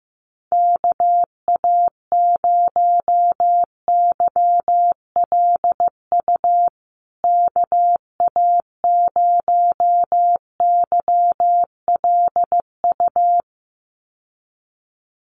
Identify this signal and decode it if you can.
Morse code, heard as a keyed tone, about 15 wpm, 700 Hz. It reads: KA0YLU KA0YLU
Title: USA Call Signs Repeated 2x 15wpm